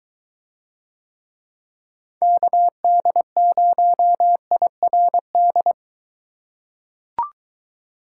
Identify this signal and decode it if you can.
Morse code, heard as a keyed tone, about 21 wpm, 700 Hz. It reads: KD0IRD